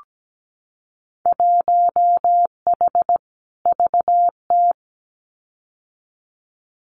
Morse code, heard as a keyed tone, about 17 wpm, 700 Hz. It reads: 1H VT
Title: Winter Field Day Exchanges 17wpm